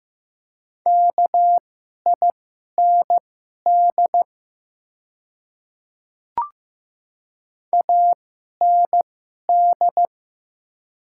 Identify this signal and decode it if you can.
Morse code, heard as a keyed tone, about 15 wpm, spaced as for 11 wpm, 700 Hz. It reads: KIND AND
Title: 2x Character Spacing - Top 300 Words - 15wpm